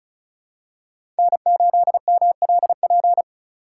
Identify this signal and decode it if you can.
Morse code, heard as a keyed tone, about 34 wpm, 700 Hz. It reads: N8MLP
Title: USA Call Signs - 5 Characters - 4x Character Spacing - 35wpm